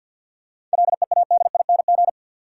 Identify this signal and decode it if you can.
Morse code, heard as a keyed tone, about 52 wpm, 700 Hz. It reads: LEADING
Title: Seven Letter Words - 50wpm